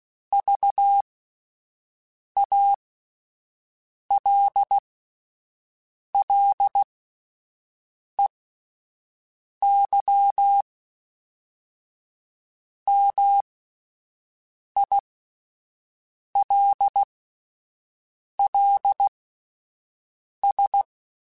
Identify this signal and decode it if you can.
Morse code, heard as a keyed tone, about 16 wpm, spaced as for 5 wpm, 800 Hz. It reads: VALLEY MILLS